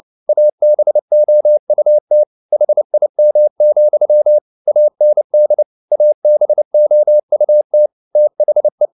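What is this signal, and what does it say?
Morse code, heard as a keyed tone, about 29 wpm, 600 Hz. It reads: ABOUT HIM, AND ABOUT THE
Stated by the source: Morse code at 29 wpm, sending Ken Follett's novel Never in English, from CW med Gnister.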